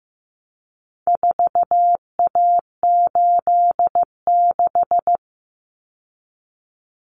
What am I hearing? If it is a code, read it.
Morse code, heard as a keyed tone, about 15 wpm, 700 Hz. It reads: 4A86
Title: Common Cut Numbers - 4-Digit 15wpm